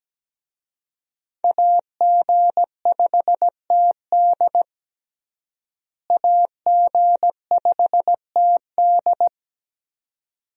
Morse code, heard as a keyed tone, about 17 wpm, 700 Hz. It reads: AG5TD AG5TD
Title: USA Call Signs - 5 Characters - Repeated 2x - 17wpm